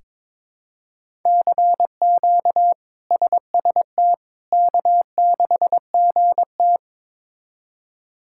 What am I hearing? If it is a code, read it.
Morse code, heard as a keyed tone, about 22 wpm, 700 Hz. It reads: CQ SST K6GT
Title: K1USN SST QSO Practice - 22wpm